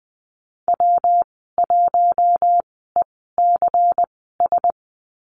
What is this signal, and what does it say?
Morse code, heard as a keyed tone, about 20 wpm, spaced as for 15 wpm, 700 Hz. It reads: W1ECS
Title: 122 Lesson 20g - Number 3 - Call Signs 20wpm